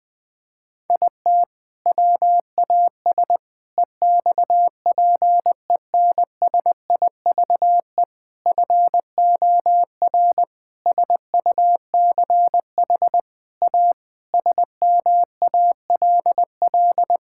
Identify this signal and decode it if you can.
Morse code, heard as a keyed tone, about 20 wpm, 700 Hz. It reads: IT WAS EXPENSIVE FOR SUCH A SMALL
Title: Sentences from Top 1000 Words - Review Part 6 - 20wpm